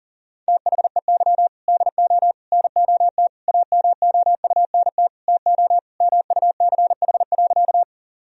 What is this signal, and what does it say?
Morse code, heard as a keyed tone, about 37 wpm, 700 Hz. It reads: THEY DO NOT AMOUNT TO MUCH.
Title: Sentences from Top 1000 Words - Review Part 4 - 40wpm